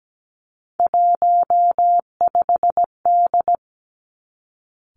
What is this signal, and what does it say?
Morse code, heard as a keyed tone, about 17 wpm, 700 Hz. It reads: 15D
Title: Three Challenging Letter-Number Combinations 17wpm